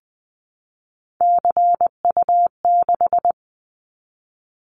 Morse code, heard as a keyed tone, about 20 wpm, 700 Hz. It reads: CU6
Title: Callsign Prefixes for Top 100 DXCC - 20wpm